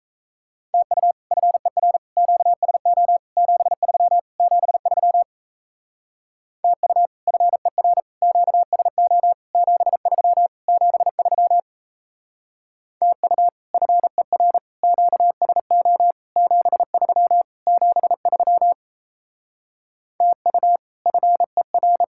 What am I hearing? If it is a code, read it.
Morse code, heard as a keyed tone, about 34 wpm, 700 Hz. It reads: TU FER QSO 73 73 TU FER QSO 73 73 TU FER QSO 73 73 TU FER
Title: Speed-Racing Common QSO Phrases - 28wpm